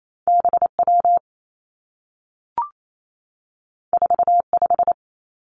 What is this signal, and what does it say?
Morse code, heard as a keyed tone, about 28 wpm, 700 Hz. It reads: B W 4 5
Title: Two Challenging Letter-Number Combinations 28wpm